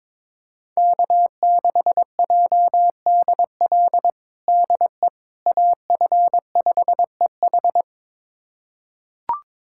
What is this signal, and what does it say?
Morse code, heard as a keyed tone, about 22 wpm, 700 Hz. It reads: K6JDL DE AF5EH
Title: USA Callsign DE Callsign - 5 Characters - 22wpm